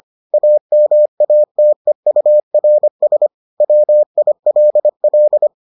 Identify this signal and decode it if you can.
Morse code, heard as a keyed tone, about 25 wpm, 600 Hz. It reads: AMATEURS WILL